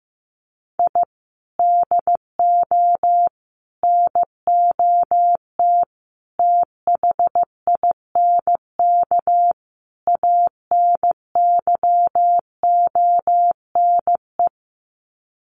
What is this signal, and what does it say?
Morse code, heard as a keyed tone, about 15 wpm, 700 Hz. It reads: I DO NOT THINK ANYONE
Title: Sets of 5 Words Encore 15wpm